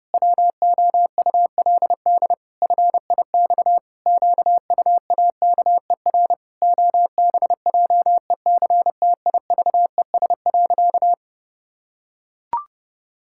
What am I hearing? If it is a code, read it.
Morse code, heard as a keyed tone, about 28 wpm, 700 Hz. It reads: WOULD FIX QUAKER OBJECTIVES.
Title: Pangram Challenge v2 - 30wpm